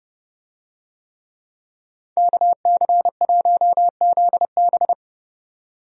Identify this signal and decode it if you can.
Morse code, heard as a keyed tone, about 30 wpm, 700 Hz. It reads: KC1ZB
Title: USA Call Signs - 5 Characters - 30wpm